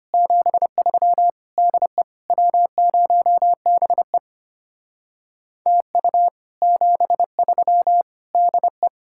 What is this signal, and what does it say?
Morse code, heard as a keyed tone, about 32 wpm, 700 Hz. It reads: 73 DE W0BE TU 73 DE